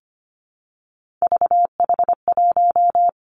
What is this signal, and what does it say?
Morse code, heard as a keyed tone, about 25 wpm, 700 Hz. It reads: VH1
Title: Three Challenging Letter-Number Combinations 25wpm